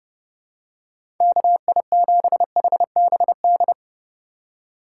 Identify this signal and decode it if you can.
Morse code, heard as a keyed tone, about 30 wpm, 700 Hz. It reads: KI7HBD
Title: USA Call Signs - 6 Characters - 3x Character Spacing - 30wpm